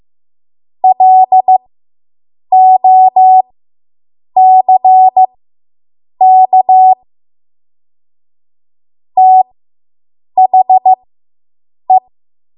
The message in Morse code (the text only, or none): LOCK THE